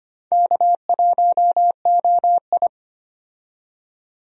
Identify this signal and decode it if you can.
Morse code, heard as a keyed tone, about 25 wpm, 700 Hz. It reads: K1OI
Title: USA Call Signs - 4 Characters - 25wpm